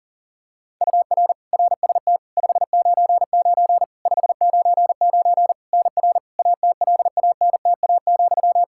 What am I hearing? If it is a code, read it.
Morse code, heard as a keyed tone, about 40 wpm, 700 Hz. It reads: UR RST 599 599 NR ATLANTA,
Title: Practice QSO - Exchange 1 - Beginning - 40wpm